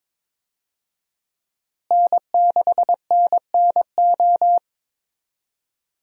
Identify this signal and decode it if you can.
Morse code, heard as a keyed tone, about 22 wpm, 700 Hz. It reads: N6NNO
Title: USA Call Signs - 5 Characters - Repeated 3x - 22wpm